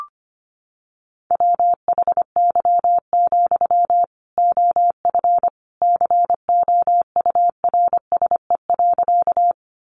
Morse code, heard as a keyed tone, about 25 wpm, 700 Hz. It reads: WHY, OF COURSE.
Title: Sentences from Top 500 Words 25wpm